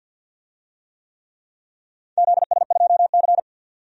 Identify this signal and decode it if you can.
Morse code, heard as a keyed tone, about 51 wpm, 700 Hz. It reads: ZS2C
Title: DXCC-100 Call Signs - 4 Characters - 50wpm